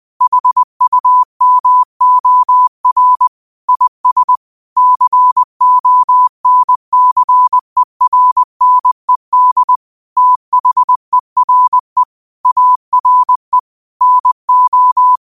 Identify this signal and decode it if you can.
Morse code, heard as a keyed tone, about 20 wpm, 1000 Hz. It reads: HUMOR IS CONCERNED THERE ARE NO